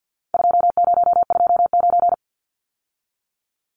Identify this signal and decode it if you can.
Morse code, heard as a keyed tone, about 50 wpm, 700 Hz. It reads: 2029
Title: Numbers - 4-Digit 50wpm